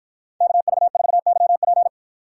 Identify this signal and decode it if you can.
Morse code, heard as a keyed tone, about 53 wpm, 700 Hz. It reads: KV4YP